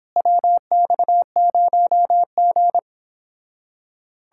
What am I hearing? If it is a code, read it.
Morse code, heard as a keyed tone, about 29 wpm, 700 Hz. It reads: WX0G